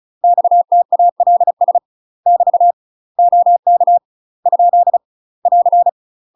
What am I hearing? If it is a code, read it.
Morse code, heard as a keyed tone, about 34 wpm, 700 Hz. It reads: XTALS <BT> OK ? <AR>
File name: UK MASTER QSOs TRAINING / 35WPM / UK MASTER QSOs 0006